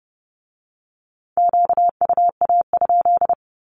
Morse code, heard as a keyed tone, about 30 wpm, 690 Hz. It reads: QUA?
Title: Q-Codes with Spoken Meaning - 30wpm